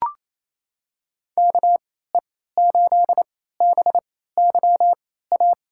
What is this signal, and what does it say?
Morse code, heard as a keyed tone, about 28 wpm, spaced as for 16 wpm, 700 Hz. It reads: KE8BYA
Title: USA Call Signs - 6 Characters - 3x Character Spacing - 28wpm